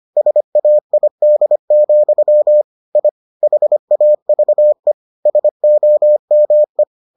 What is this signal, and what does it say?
Morse code, heard as a keyed tone, about 25 wpm, 600 Hz. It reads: SAID, I HAVE SOME